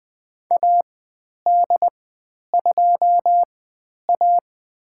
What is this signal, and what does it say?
Morse code, heard as a keyed tone, about 20 wpm, spaced as for 10 wpm, 700 Hz. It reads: AD2A
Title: 095 Lesson 16g - Letter c - Call Signs 20wpm